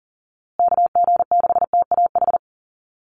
Morse code, heard as a keyed tone, about 40 wpm, 700 Hz. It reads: KG6TAH
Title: USA Call Signs Repeated 3x 40wpm